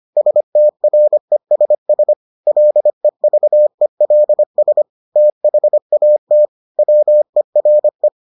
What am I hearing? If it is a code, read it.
Morse code, heard as a keyed tone, about 25 wpm, 600 Hz. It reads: STRESS LEVELS THAT WERE